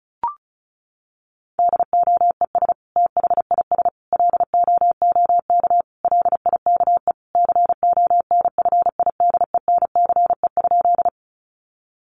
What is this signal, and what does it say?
Morse code, heard as a keyed tone, about 34 wpm, 700 Hz. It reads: DOES THIS LOOK LIKE CONFIDENCE?
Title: Sentences from Top 1500 Words 35wpm